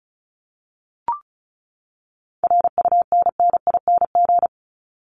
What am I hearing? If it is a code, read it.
Morse code, heard as a keyed tone, about 34 wpm, 700 Hz. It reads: RUNNING
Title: ING Suffix - Top 100 Words with Base Words 35wpm